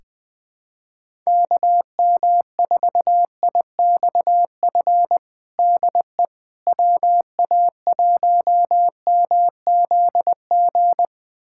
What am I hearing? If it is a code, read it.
Morse code, heard as a keyed tone, about 20 wpm, 700 Hz. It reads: KM4IXF DE WA1MZG
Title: USA Callsign DE Callsign 20wpm